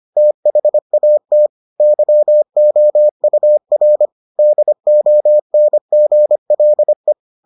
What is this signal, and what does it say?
Morse code, heard as a keyed tone, about 25 wpm, 600 Hz. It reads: THAT YOUR DONGLE